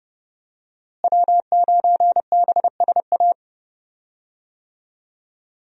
Morse code, heard as a keyed tone, about 30 wpm, 700 Hz. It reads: W9BSA
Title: USA Call Signs - 5 Characters - Repeated 2x - 30wpm